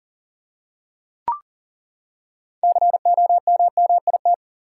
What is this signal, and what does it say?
Morse code, heard as a keyed tone, about 40 wpm, 700 Hz. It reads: COMMIT
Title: Six Letter Words - 40wpm